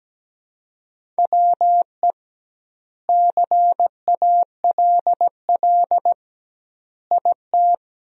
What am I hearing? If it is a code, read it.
Morse code, heard as a keyed tone, about 17 wpm, 700 Hz. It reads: WE CALL IT
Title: 2x Word-Spacing - Sentences from Top 400 Words - 17wpm